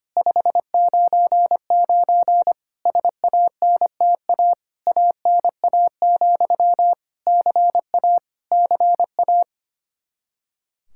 Morse code, heard as a keyed tone, about 25 wpm, 700 Hz. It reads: 599 SANTA ANA, CA CA
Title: Practice QSO - Exchange 1 - Beginning - 25wpm